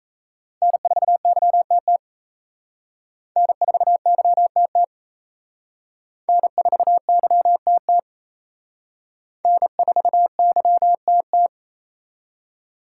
Morse code, heard as a keyed tone, about 37 wpm, 700 Hz. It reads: N4YTT N4YTT N4YTT N4YTT